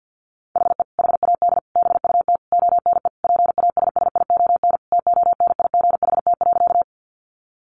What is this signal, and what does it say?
Morse code, heard as a keyed tone, about 50 wpm, 700 Hz. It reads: HE HAD BUT ONE PASSION TONIGHT.
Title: Sentences from Top 1300 Words 50wpm